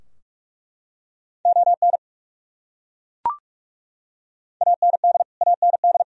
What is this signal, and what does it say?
Morse code, heard as a keyed tone, about 41 wpm, 700 Hz. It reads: ON AND AND
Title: Top 100 Words Repeated 45wpm